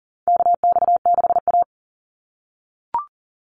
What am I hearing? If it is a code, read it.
Morse code, heard as a keyed tone, about 42 wpm, 700 Hz. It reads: KX6A
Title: NA SOTA Chaser Callsigns 40wpm